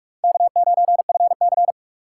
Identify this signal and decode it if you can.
Morse code, heard as a keyed tone, about 46 wpm, 700 Hz. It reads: K9FC
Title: USA Call Signs - 4 Characters - 4x Character Spacing - 45wpm